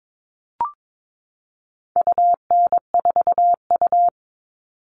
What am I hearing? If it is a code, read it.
Morse code, heard as a keyed tone, about 22 wpm, 700 Hz. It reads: UN4U
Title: All Cut Numbers - 4-Digit 22wpm